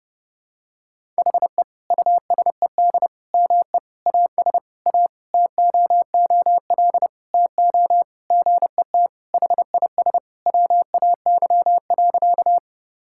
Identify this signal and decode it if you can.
Morse code, heard as a keyed tone, about 32 wpm, 700 Hz. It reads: HE USED ME AS A TOOL TO GET HIS WAY.
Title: Sentences from Top 1000 Words - Review Part 6 - 30wpm